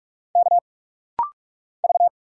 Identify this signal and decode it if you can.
Morse code, heard as a keyed tone, about 46 wpm, 700 Hz. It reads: K V
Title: Single Letter-Number Alt - Rapid-Fire 45wpm